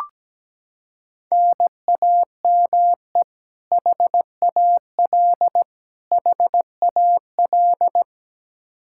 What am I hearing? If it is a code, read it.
Morse code, heard as a keyed tone, about 17 wpm, 700 Hz. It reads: NAME HAL HAL
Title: Practice QSO - Exchange 1 - Beginning - 17wpm